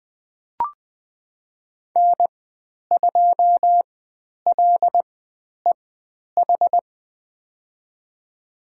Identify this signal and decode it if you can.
Morse code, heard as a keyed tone, about 20 wpm, spaced as for 10 wpm, 700 Hz. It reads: N2LEH
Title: 095 Lesson 16g - Letter c - Call Signs 20wpm